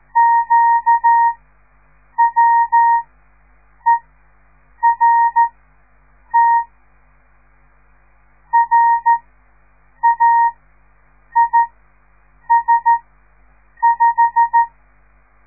Morse code, heard as a keyed tone, about 13 wpm, spaced as for 7 wpm, 930 Hz. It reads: QWERT RAIS5
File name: radiomorningstar / morse / les 6